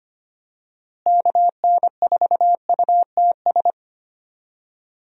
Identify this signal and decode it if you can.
Morse code, heard as a keyed tone, about 25 wpm, 700 Hz. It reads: KN4UTS
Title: USA Call Signs - 6 Characters - Repeated 3x - 25wpm